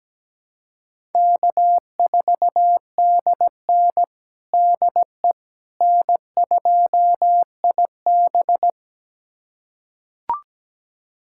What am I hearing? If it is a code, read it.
Morse code, heard as a keyed tone, about 17 wpm, 700 Hz. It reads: K4DN DE N2IB
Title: USA Callsign DE Callsign - 4 Characters - 17wpm